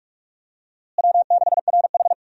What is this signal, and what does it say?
Morse code, heard as a keyed tone, about 46 wpm, 700 Hz. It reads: W6RH
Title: USA Call Signs - 4 Characters - Repeated 3x - 45wpm